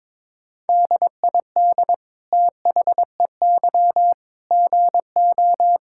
Morse code, heard as a keyed tone, about 21 wpm, 700 Hz. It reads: DID THEY GO
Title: Sentences from Top 1000 Words - Review Part 5 - 22wpm